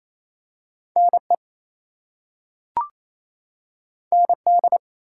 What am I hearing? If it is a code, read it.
Morse code, heard as a keyed tone, about 28 wpm, 700 Hz. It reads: NE ND
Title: US States and Canadian Provinces 28wpm